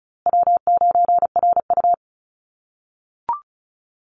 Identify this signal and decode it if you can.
Morse code, heard as a keyed tone, about 34 wpm, 700 Hz. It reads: W9RU
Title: USA Call Signs Repeated 2x 35wpm